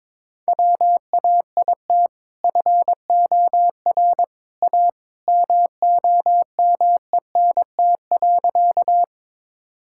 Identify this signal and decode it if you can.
Morse code, heard as a keyed tone, about 22 wpm, 700 Hz. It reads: WAIT FOR A MOMENT.
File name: practice / 22wpm / Sentences with 2-Word Phrases - 22wpm